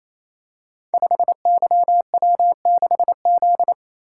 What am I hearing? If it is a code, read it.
Morse code, heard as a keyed tone, about 27 wpm, 700 Hz. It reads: 5YW6Z
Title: Code Groups - 5 Letter:Numbers - 28wpm